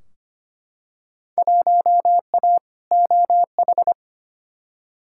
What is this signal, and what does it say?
Morse code, heard as a keyed tone, about 25 wpm, 700 Hz. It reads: 1A OH
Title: Field Day Exchanges - 25wpm